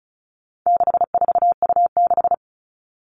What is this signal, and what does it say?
Morse code, heard as a keyed tone, about 34 wpm, 700 Hz. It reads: 64U6
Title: All Cut Numbers - 4-Digit 35wpm